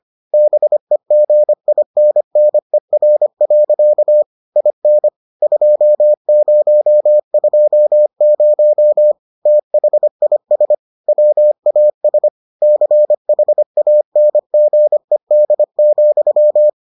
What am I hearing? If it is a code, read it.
Morse code, heard as a keyed tone, about 25 wpm, 600 Hz. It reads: BEGINNER. IN 2020 THIS WAS CHANGED,